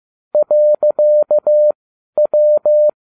Morse code, heard as a keyed tone, about 15 wpm, spaced as for 11 wpm, 600 Hz. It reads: .W
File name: cw / ubacw12 2